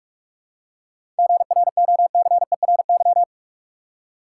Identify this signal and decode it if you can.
Morse code, heard as a keyed tone, about 43 wpm, 700 Hz. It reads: GROCERY
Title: Seven Letter Words - 45wpm